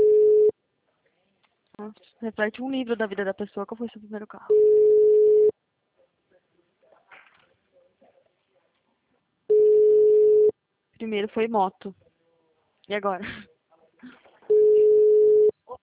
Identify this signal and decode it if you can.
Ringback tone (Europe, 425 Hz), with speech.